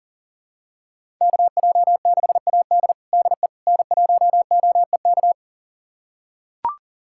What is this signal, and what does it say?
Morse code, heard as a keyed tone, about 40 wpm, 700 Hz. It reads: KJ6AD DE N1OEK
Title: USA Callsign DE Callsign - 5 Characters - 40wpm